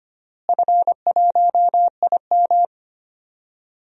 Morse code, heard as a keyed tone, about 25 wpm, 700 Hz. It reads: F1IM
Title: DXCC-100 Call Signs - 4 Characters - Repeated 3x - 25wpm